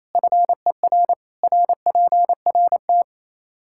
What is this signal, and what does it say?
Morse code, heard as a keyed tone, about 28 wpm, 700 Hz. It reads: FER RPRT
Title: Practice QSO - Exchange 2 - Info - 28wpm